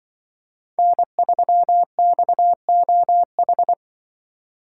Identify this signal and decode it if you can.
Morse code, heard as a keyed tone, about 23 wpm, 700 Hz. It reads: N3XOH
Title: Speed Racing - USA 1x3 Call Signs 20wpm